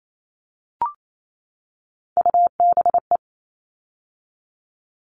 Morse code, heard as a keyed tone, about 28 wpm, 700 Hz. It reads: UBE
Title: All Cut Numbers - 3-Digit 28wpm